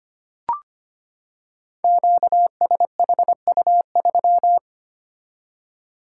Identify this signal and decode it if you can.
Morse code, heard as a keyed tone, about 25 wpm, 700 Hz. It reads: QSHU3
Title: Code Groups - 5 Letter:Numbers - 25wpm